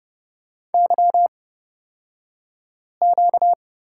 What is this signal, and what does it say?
Morse code, heard as a keyed tone, about 31 wpm, spaced as for 5 wpm, 700 Hz. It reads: YQ
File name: Lesson 38d - Letter q - Two Characters 30wpm